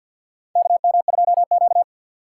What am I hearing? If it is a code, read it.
Morse code, heard as a keyed tone, about 53 wpm, 700 Hz. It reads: KM2Q
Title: USA Call Signs - 4 Characters - 2x Character Spacing - 50wpm